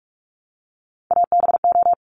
Morse code, heard as a keyed tone, about 46 wpm, 690 Hz. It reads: ABO
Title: Top Three Letter Combinations in Words 45wpm